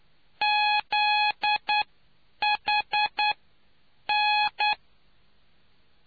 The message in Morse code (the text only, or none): ZHN